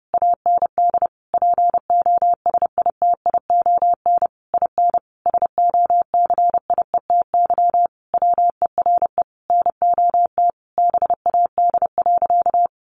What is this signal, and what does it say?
Morse code, heard as a keyed tone, about 28 wpm, 700 Hz. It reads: AND POSITION IN SOCIETY WERE NOT BAD.
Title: Sentences from Top 900 Words 30wpm